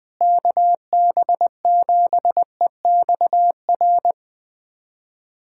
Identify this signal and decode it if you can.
Morse code, heard as a keyed tone, about 20 wpm, 700 Hz. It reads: KB7EXR
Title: USA Callsign DE Callsign 20wpm